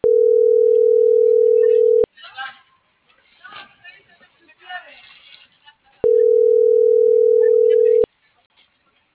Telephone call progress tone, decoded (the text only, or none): ringback tone